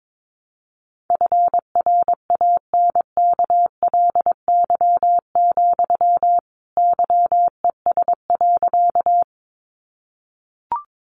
Morse code, heard as a keyed tone, about 22 wpm, 700 Hz. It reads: FRANKLY, YES.
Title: A Sound Of Thunder 22wpm